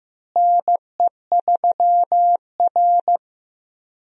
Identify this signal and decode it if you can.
Morse code, heard as a keyed tone, about 15 wpm, 700 Hz. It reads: NE3R
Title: USA Call Signs - 4 Characters - Repeated 2x - 15wpm